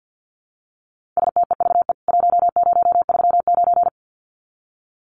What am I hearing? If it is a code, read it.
Morse code, heard as a keyed tone, about 50 wpm, 700 Hz. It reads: STEVE 1039
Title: CWT Exchange 50wpm